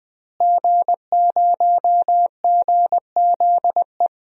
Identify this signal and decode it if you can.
Morse code, heard as a keyed tone, about 21 wpm, 700 Hz. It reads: G0GZE
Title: DXCC-100 Call Signs - 5 Characters - Repeated 3x - 20wpm